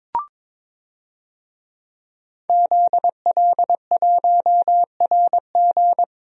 Morse code, heard as a keyed tone, about 21 wpm, 700 Hz. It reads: ZL1RG
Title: DXCC-100 Call Signs - 5 Characters - Repeated 2x - 22wpm